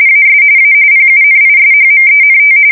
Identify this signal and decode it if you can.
RTTY, about 45.45 baud. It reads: AS A PUNISHMENT
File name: fsk170 / usnavy2